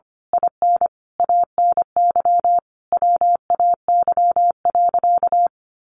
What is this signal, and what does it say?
Morse code, heard as a keyed tone, about 25 wpm, 700 Hz. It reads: IN ANY WAY.
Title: A Sound Of Thunder 25wpm